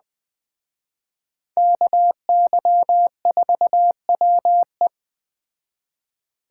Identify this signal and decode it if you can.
Morse code, heard as a keyed tone, about 21 wpm, 700 Hz. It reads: KY4WE
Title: Speed Racing - USA Call Signs 17wpm